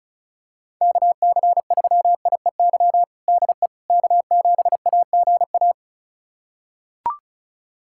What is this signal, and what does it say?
Morse code, heard as a keyed tone, about 36 wpm, 700 Hz. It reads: KC3IEY DE K7AGA
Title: USA Callsign DE Callsign 35wpm